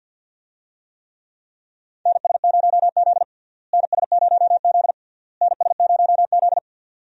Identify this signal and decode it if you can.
Morse code, heard as a keyed tone, about 46 wpm, 700 Hz. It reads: NS0Z NS0Z NS0Z
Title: USA Call Signs - 4 Characters - Repeated 3x - 50wpm